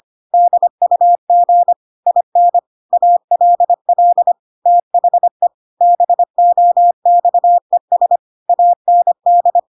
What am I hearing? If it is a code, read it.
Morse code, heard as a keyed tone, about 25 wpm, 700 Hz. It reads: DUG IN ALL THE BOXES AND
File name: Pinocchio / Pinocchio ch5 25 25 ik1hge